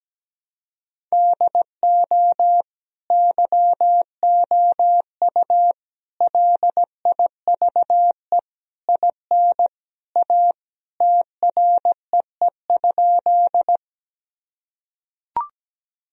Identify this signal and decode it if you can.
Morse code, heard as a keyed tone, about 17 wpm, 700 Hz. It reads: DO YOU LIVE IN A TREE?
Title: Sentences from Top 1000 Words - Review Part 1 - 17wpm